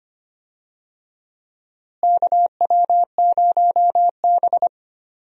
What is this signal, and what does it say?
Morse code, heard as a keyed tone, about 25 wpm, 700 Hz. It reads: KW0B
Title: USA Call Signs - 4 Characters - Repeated 3x - 25wpm